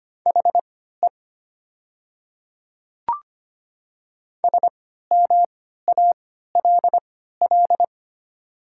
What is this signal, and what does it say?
Morse code, heard as a keyed tone, about 25 wpm, spaced as for 14 wpm, 700 Hz. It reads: HE SMALL
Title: 3x Character Spacing - Top 200 Words - 25wpm